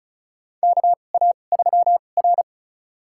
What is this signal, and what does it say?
Morse code, heard as a keyed tone, about 36 wpm, spaced as for 25 wpm, 700 Hz. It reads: KA3R
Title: USA Call Signs - 2x Character Spacing - 35wpm